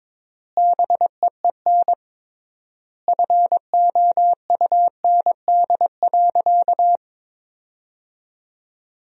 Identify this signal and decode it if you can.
Morse code, heard as a keyed tone, about 21 wpm, 700 Hz. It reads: BEEN FOUND.
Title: 3x Word-Spacing - Sentences from Top 300 Words - 22wpm